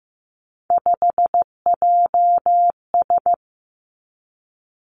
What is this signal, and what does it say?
Morse code, heard as a keyed tone, about 15 wpm, 700 Hz. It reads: 5JS
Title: Three Challenging Letter-Number Combinations 15wpm